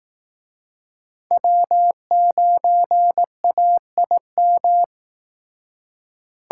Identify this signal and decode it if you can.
Morse code, heard as a keyed tone, about 19 wpm, 700 Hz. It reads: W9AIM